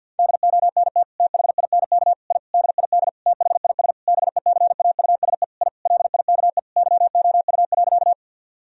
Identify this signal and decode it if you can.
Morse code, heard as a keyed tone, about 50 wpm, 700 Hz. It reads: DONT THINK I DID THIS BECAUSE I LIKE YOU.
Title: Sentences from Top 1000 Words - Review Part 3 - 50wpm